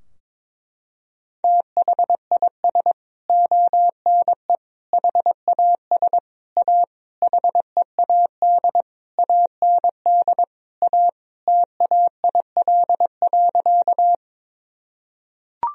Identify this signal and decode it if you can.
Morse code, heard as a keyed tone, about 22 wpm, 700 Hz. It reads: THIS ONE HAS A HEAD AND A TAIL.
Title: Sentences from Top 1000 Words - Review Part 3 - 22wpm